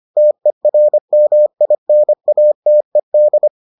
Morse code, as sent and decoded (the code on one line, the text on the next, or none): - . .-. -- .. -. .- - . -..
TERMINATED